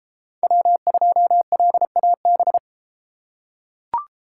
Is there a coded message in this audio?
Yes: Morse code from 0.4 to 2.6 s, heard as a keyed tone, about 33 wpm, 700 Hz. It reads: W2LAB